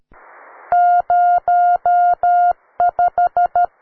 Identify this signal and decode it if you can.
Morse code, heard as a keyed tone, about 13 wpm, 710 Hz. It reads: 05
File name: wae / cw beginner / ua9cdv cw